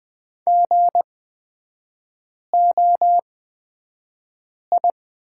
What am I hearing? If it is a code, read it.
Morse code, heard as a keyed tone, about 20 wpm, spaced as for 5 wpm, 700 Hz. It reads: GOI
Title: 224 Lesson 36e - BT - Three Characters 20wpm